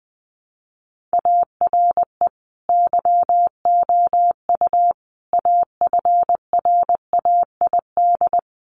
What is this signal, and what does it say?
Morse code, heard as a keyed tone, about 20 wpm, 700 Hz. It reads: ARE YOU AFRAID
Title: Sentences from Top 1500 Words 20wpm